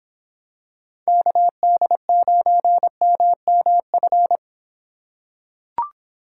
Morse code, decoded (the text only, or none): KD9MMF